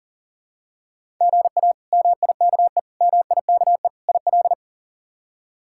Morse code, heard as a keyed tone, about 40 wpm, 700 Hz. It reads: GA MIKE MIKE IL
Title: K1USN SST QSO Practice - 40wpm